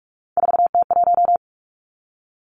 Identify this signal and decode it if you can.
Morse code, heard as a keyed tone, about 46 wpm, 700 Hz. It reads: 4T1
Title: Three Challenging Letter-Number Combinations 45wpm